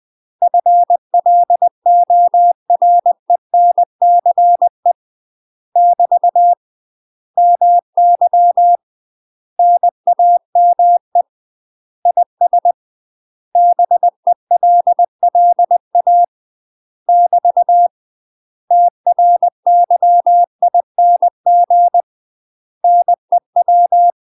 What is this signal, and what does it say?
Morse code, heard as a keyed tone, about 20 wpm, 700 Hz. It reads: FLORENCE <BT> MY NAME IS BELLA <BT> TRYING NEW
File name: UK MASTER QSOs TRAINING / 20WPM / UK MASTER QSOs 0068